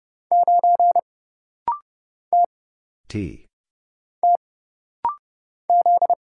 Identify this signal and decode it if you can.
Morse code, heard as a keyed tone, about 30 wpm, 700 Hz. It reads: 9 T T Z
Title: Single Letter-Number - Rapid-Fire 30wpm